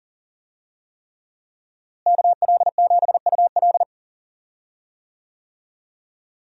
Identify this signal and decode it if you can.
Morse code, heard as a keyed tone, about 40 wpm, 700 Hz. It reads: KL7UL
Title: USA Call Signs - 5 Characters - Repeated 3x - 40wpm